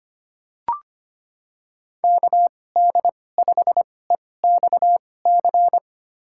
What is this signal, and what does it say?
Morse code, heard as a keyed tone, about 25 wpm, spaced as for 18 wpm, 700 Hz. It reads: KD5EXC
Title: USA Call Signs - 2x Character Spacing - 25wpm